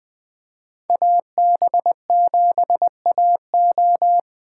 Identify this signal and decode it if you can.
Morse code, heard as a keyed tone, about 20 wpm, 700 Hz. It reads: AB7AO
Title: USA Call Signs - 3x Character Spacing - 20wpm